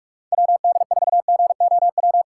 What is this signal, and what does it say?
Morse code, heard as a keyed tone, about 46 wpm, 700 Hz. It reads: WD4GOW
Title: USA Call Signs - 6 Characters - 3x Character Spacing - 45wpm